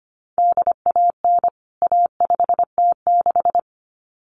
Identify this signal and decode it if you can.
Morse code, heard as a keyed tone, about 25 wpm, 700 Hz. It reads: DAN A5T6
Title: CWT Exchange 25wpm